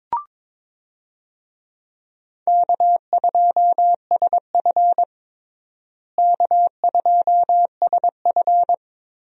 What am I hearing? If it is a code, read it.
Morse code, heard as a keyed tone, about 22 wpm, 700 Hz. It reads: K2SF K2SF
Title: USA Call Signs - 4 Characters - Repeated 3x - 22wpm